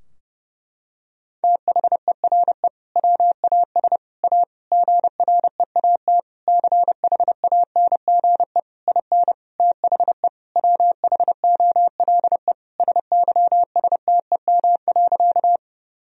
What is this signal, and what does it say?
Morse code, heard as a keyed tone, about 30 wpm, 700 Hz. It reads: THERE WAS A GREAT CHANGE IN THE WHOLE SYSTEM.
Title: Sentences from Top 1000 Words - Review Part 3 - 30wpm